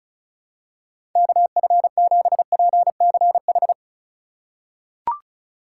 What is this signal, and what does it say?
Morse code, heard as a keyed tone, about 36 wpm, 700 Hz. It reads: KF7PCH
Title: USA Call Signs - 4x Character Spacing - 35wpm